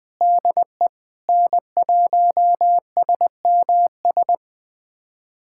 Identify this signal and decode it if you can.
Morse code, heard as a keyed tone, about 20 wpm, 700 Hz. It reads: DE N1SMS
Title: USA Callsign DE Callsign - 5 Characters - 20wpm